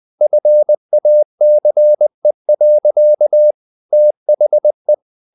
Morse code, heard as a keyed tone, about 20 wpm, 600 Hz. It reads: FACE. THE
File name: the old man and the sea / the old man and the sea part 1 20wpm